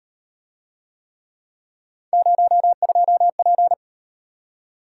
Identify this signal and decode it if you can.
Morse code, heard as a keyed tone, about 38 wpm, 700 Hz. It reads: 02P